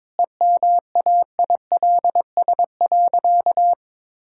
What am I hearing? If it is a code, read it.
Morse code, heard as a keyed tone, about 22 wpm, 700 Hz. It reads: EMAILS.
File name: practice / 22wpm / Easy Sentences with Hints - 22wpm